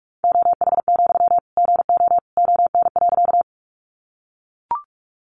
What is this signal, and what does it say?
Morse code, heard as a keyed tone, about 44 wpm, 700 Hz. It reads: OH, GO ON.
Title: Sentences from Top 500 Words 45wpm